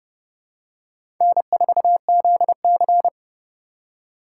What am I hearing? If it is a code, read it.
Morse code, heard as a keyed tone, about 30 wpm, 700 Hz. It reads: N4ZC
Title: USA Call Signs - 4 Characters - 4x Character Spacing - 30wpm